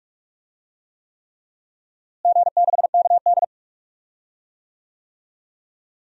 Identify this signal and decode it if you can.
Morse code, heard as a keyed tone, about 42 wpm, 700 Hz. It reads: G6KD
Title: DXCC-100 Call Signs - 4 Characters - Repeated 3x - 45wpm